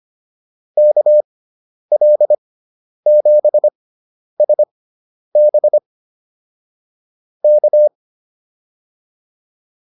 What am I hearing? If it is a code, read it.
Morse code, heard as a keyed tone, about 25 wpm, spaced as for 10 wpm, 600 Hz. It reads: KL7SB K
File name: QSO107 10